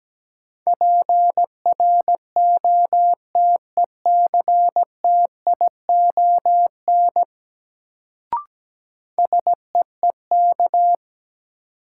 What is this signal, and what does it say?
Morse code, heard as a keyed tone, about 17 wpm, 700 Hz. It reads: PROTECTION SEEK
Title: Sentences from Top 1600 Words - New Word Drill Alt - 17wpm